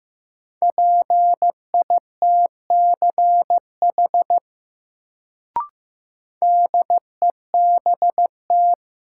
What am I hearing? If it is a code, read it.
Morse code, heard as a keyed tone, about 15 wpm, 700 Hz. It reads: PITCH DEBT
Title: Sentences from Top 1300 Words - New Word Drill Alt - 15wpm